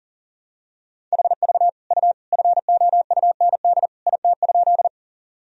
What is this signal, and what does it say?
Morse code, heard as a keyed tone, about 40 wpm, 700 Hz. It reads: HV U FOUND IT?
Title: Sentences from Top 300 Words and Abbreviations 40wpm